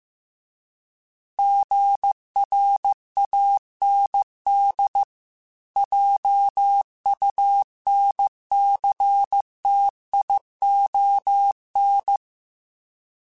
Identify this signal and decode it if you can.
Morse code, heard as a keyed tone, about 15 wpm, 800 Hz. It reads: GRAND JUNCTION